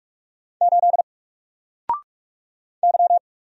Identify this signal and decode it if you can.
Morse code, heard as a keyed tone, about 44 wpm, 700 Hz. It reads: 8 Y
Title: Single Character (Letter:Number:Punctuation:Prosign) Alt 45wpm